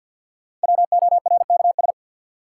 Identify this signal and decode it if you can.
Morse code, heard as a keyed tone, about 52 wpm, 700 Hz. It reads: WORKS
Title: Sentences from Top 1000 Words - Review Part 5 - New Word Drill - 50wpm